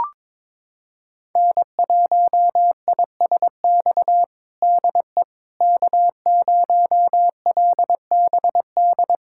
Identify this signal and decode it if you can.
Morse code, heard as a keyed tone, about 22 wpm, 700 Hz. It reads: N1ISX DE K0LBD
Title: USA Callsign DE Callsign - 5 Characters - 22wpm